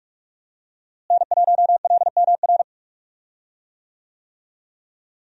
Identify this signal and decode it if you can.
Morse code, heard as a keyed tone, about 46 wpm, 700 Hz. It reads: N1LMR
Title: USA Call Signs - 5 Characters - Repeated 2x - 45wpm